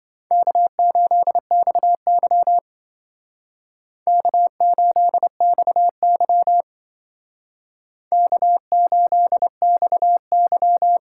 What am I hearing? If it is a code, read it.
Morse code, heard as a keyed tone, about 26 wpm, 700 Hz. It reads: K8XY K8XY K8XY